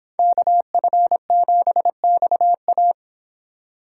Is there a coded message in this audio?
Yes: Morse code from 0.2 to 2.9 s, heard as a keyed tone, about 31 wpm, 700 Hz. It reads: KF7XA